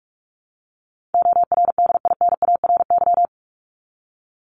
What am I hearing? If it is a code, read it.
Morse code, heard as a keyed tone, about 44 wpm, 700 Hz. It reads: ORDINARY
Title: Top 1751-2000 Words 45wpm